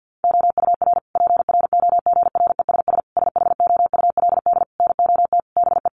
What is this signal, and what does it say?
Morse code, heard as a keyed tone, about 50 wpm, 700 Hz. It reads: OUR PROGRESS SHOULD NOT BE
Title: Sentences from Top 1400 Words 50wpm